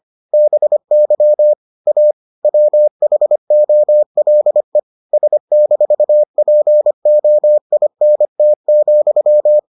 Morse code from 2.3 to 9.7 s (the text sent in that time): WHOLE S-POINT,